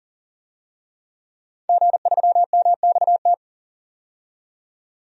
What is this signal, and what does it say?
Morse code, heard as a keyed tone, about 40 wpm, 700 Hz. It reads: G3MXT
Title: DXCC-100 Call Signs - 5 Characters - 40wpm